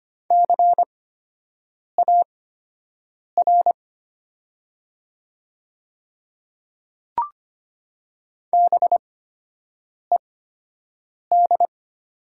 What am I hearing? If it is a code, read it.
Morse code, heard as a keyed tone, about 25 wpm, spaced as for 7 wpm, 700 Hz. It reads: CAR BED
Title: 8x Character Spacing - Top 500 Words - 25wpm